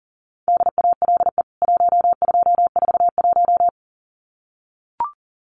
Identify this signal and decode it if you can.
Morse code, heard as a keyed tone, about 40 wpm, 700 Hz. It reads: DALE 1241
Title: CWT Exchange 40wpm